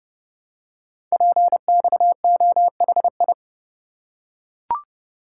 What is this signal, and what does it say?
Morse code, heard as a keyed tone, about 30 wpm, 700 Hz. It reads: PXOHI
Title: Code Groups - 5 Letters - 30wpm